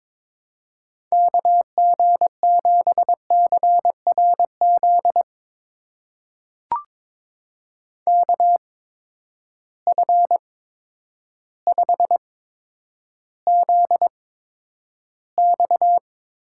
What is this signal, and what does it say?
Morse code, heard as a keyed tone, about 22 wpm, spaced as for 10 wpm, 700 Hz. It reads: KG7CRZ KF5ZX
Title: USA Call Signs - 6 Characters - 8x Character Spacing - 22wpm